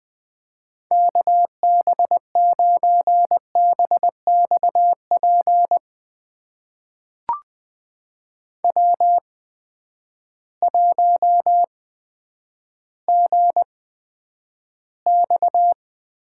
Morse code, heard as a keyed tone, about 20 wpm, spaced as for 10 wpm, 700 Hz. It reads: KB9BXP W1GX
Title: USA Call Signs - 8x Character Spacing - 20wpm